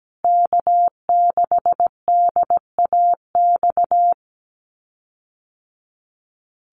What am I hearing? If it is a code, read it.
Morse code, heard as a keyed tone, about 17 wpm, 700 Hz. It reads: K6DAX
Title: USA Call Signs Repeated 3x 17wpm